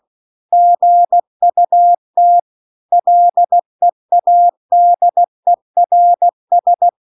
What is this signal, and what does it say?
Morse code, heard as a keyed tone, about 16 wpm, 700 Hz. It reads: GUT LEADERS
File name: CW-The-Big-Twoheaded-River-2-16WPM